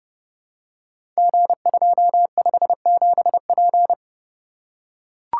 Morse code, heard as a keyed tone, about 30 wpm, 700 Hz. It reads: G257P
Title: Code Groups - 5 Letter:Numbers - 30wpm